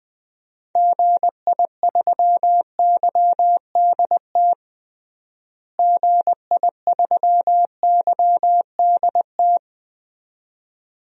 Morse code, heard as a keyed tone, about 21 wpm, 700 Hz. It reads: GI3YDT GI3YDT
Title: DXCC-100 Call Signs - 6 Characters - Repeated 2x - 20wpm